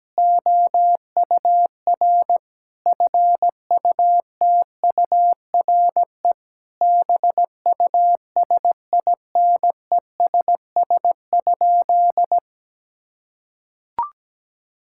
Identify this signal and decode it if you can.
Morse code, heard as a keyed tone, about 17 wpm, 700 Hz. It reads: OUR FUTURE BUSINESS?
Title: Sentences from Top 1000 Words - Review Part 6 - 17wpm